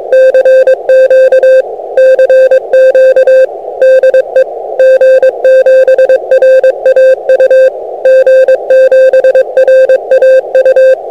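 Morse code, heard as a keyed tone, about 22 wpm, 520 Hz. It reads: CQ CQ DE G7RAU G7RAU
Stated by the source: CW from G7RAU (Dave) heard via year-end VHF tropospheric ducting, received by DH8BQA in Cologne (JO30).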